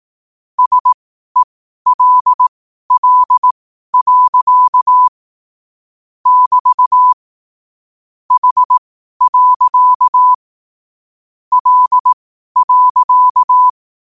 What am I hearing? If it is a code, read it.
Morse code, heard as a keyed tone, about 18 wpm, spaced as for 12 wpm, 1000 Hz. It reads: SELL. <BT> H. L.
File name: qotdfiles / 12wpm